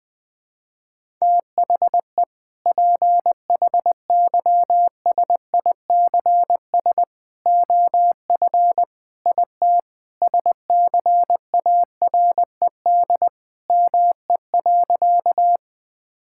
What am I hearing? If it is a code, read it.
Morse code, heard as a keyed tone, about 20 wpm, 700 Hz. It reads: THE PHYSICS OF IT SCARED ME.
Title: Sentences from Top 1000 Words - Review Part 5 - 20wpm